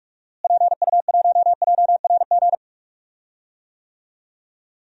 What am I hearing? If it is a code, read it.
Morse code, heard as a keyed tone, about 46 wpm, 700 Hz. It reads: PU1JRG
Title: DXCC-100 Call Signs - 6 Characters - Repeated 2x - 45wpm